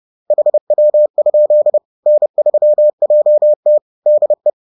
Morse code, heard as a keyed tone, about 30 wpm, 600 Hz. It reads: HW? N3JT DE
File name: 30wpm-4